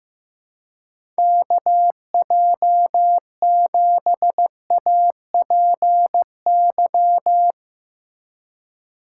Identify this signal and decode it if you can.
Morse code, heard as a keyed tone, about 15 wpm, 700 Hz. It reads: KJ7APY
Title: USA Call Signs - 6 Characters - 3x Character Spacing - 15wpm